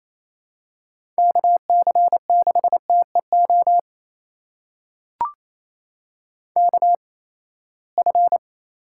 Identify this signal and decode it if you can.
Morse code, heard as a keyed tone, about 28 wpm, spaced as for 19 wpm, 700 Hz. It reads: KC6TEO KF
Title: USA Call Signs - 6 Characters - 8x Character Spacing - 28wpm